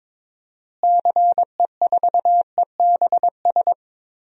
Morse code, heard as a keyed tone, about 22 wpm, 700 Hz. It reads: CE4EBS
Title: DXCC-100 Call Signs - 6 Characters - Repeated 2x - 22wpm